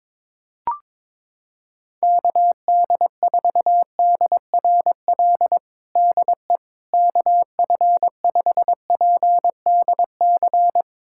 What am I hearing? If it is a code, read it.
Morse code, heard as a keyed tone, about 22 wpm, 700 Hz. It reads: KD4DRL DE KF5PDC
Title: USA Callsign DE Callsign - 6 Characters - 22wpm